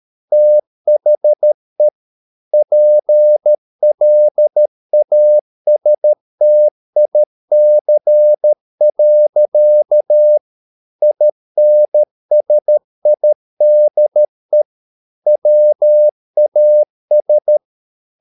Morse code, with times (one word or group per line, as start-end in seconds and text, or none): 0.3-1.9 THE
2.5-10.4 PLASTIC.
11.0-14.6 INSIDE
15.3-17.6 WAS